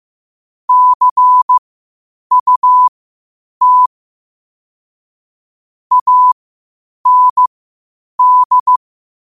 Morse code, heard as a keyed tone, about 15 wpm, spaced as for 8 wpm, 1000 Hz. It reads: CUT AND
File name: qotdfiles / 8wpm